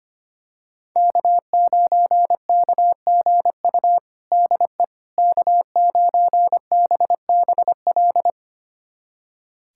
Morse code, heard as a keyed tone, about 25 wpm, 700 Hz. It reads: K9KGU DE K9BBL
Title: USA Callsign DE Callsign - 5 Characters - 25wpm